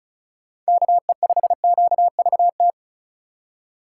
Morse code, heard as a keyed tone, about 36 wpm, 700 Hz. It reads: KE5QVT
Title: USA Call Signs - 6 Characters - 2x Character Spacing - 35wpm